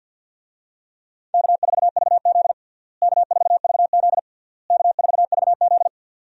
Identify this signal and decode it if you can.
Morse code, heard as a keyed tone, about 49 wpm, 700 Hz. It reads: K4VZ K4VZ K4VZ
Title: USA Call Signs - 4 Characters - Repeated 3x - 50wpm